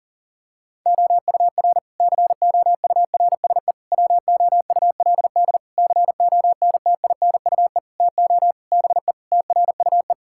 Morse code, heard as a keyed tone, about 40 wpm, 700 Hz. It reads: OUR COURSE WOULD CONTINUE TO BE TRUE
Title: Sentences from Top 1000 Words - Review Part 6 - 40wpm